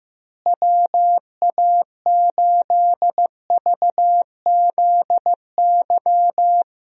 Morse code, heard as a keyed tone, about 16 wpm, 700 Hz. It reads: WA8VZY
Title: USA Call Signs - 6 Characters - Repeated 3x - 15wpm